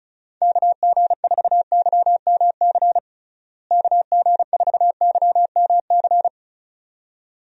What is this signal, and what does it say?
Morse code, heard as a keyed tone, about 34 wpm, 700 Hz. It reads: KG4YMC KG4YMC
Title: USA Call Signs - 6 Characters - Repeated 2x - 35wpm